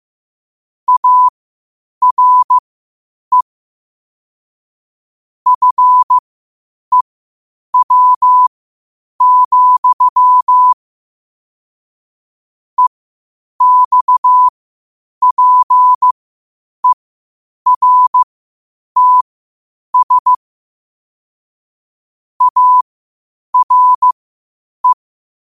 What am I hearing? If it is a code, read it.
Morse code, heard as a keyed tone, about 15 wpm, spaced as for 8 wpm, 1000 Hz. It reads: ARE FEW, EXPERTS ARE
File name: qotdfiles / 8wpm